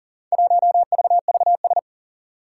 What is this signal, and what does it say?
Morse code, heard as a keyed tone, about 40 wpm, 700 Hz. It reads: 1VVS
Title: Four Challenging Letter-Number Combinations 40wpm